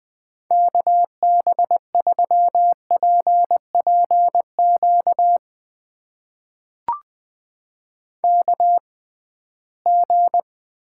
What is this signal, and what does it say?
Morse code, heard as a keyed tone, about 20 wpm, spaced as for 15 wpm, 700 Hz. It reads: KB3PPQ KG
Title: USA Call Signs - 6 Characters - 6x Character Spacing - 20wpm